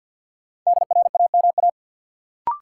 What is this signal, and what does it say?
Morse code, heard as a keyed tone, about 53 wpm, 700 Hz. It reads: DRAMA
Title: Sentences from Top 1500 Words - New Word Drill Alt - 50wpm